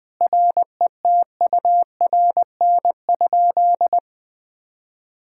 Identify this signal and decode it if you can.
Morse code, heard as a keyed tone, about 20 wpm, 700 Hz. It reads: RETURN?
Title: Sentences from Top 1600 Words 20wpm